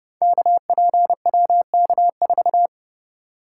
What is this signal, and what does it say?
Morse code, heard as a keyed tone, about 32 wpm, 700 Hz. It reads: KPWK4
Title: Code Groups - 5 Letter:Numbers - 30wpm